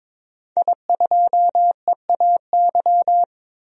Morse code, heard as a keyed tone, about 22 wpm, 700 Hz. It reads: I2EAY
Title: DXCC-100 Call Signs - 5 Characters - 22wpm